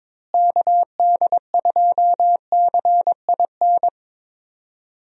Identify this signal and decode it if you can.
Morse code, heard as a keyed tone, about 22 wpm, 700 Hz. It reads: KD2CIN
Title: USA Call Signs - 6 Characters - Repeated 3x - 22wpm